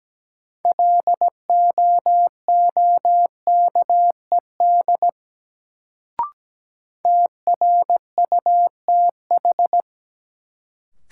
Morse code, heard as a keyed tone, about 17 wpm, 700 Hz. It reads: LOOKED TRUTH
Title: Sentences from Top 800 Words - New Word Drill - 17wpm